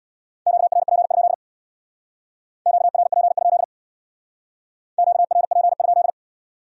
Morse code, heard as a keyed tone, about 63 wpm, 700 Hz. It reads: QRP? QRP? QRP?